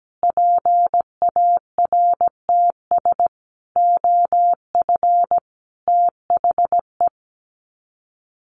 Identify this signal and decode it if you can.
Morse code, heard as a keyed tone, about 17 wpm, 700 Hz. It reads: PARTS OF THE
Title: Sets of 5 Words Encore 17wpm